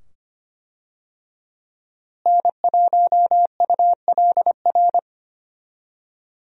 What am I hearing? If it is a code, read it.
Morse code, heard as a keyed tone, about 25 wpm, 700 Hz. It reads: N1ULR
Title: USA Call Signs - 5 Characters - Repeated 2x - 25wpm